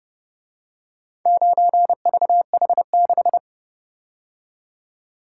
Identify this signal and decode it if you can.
Morse code, heard as a keyed tone, about 29 wpm, 700 Hz. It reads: 9VH6